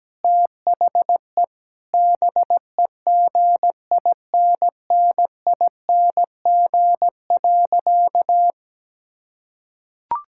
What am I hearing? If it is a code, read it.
Morse code, heard as a keyed tone, about 17 wpm, 700 Hz. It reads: THE BEGINNING.
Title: Sentences from Top 1000 Words - Review Part 5 - 17wpm